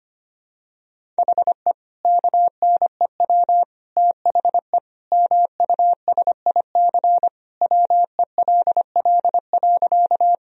Morse code, heard as a keyed tone, about 25 wpm, 700 Hz. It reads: HE KNEW THE MUSIC WELL.
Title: Sentences from Top 1000 Words - Review Part 1 - 25wpm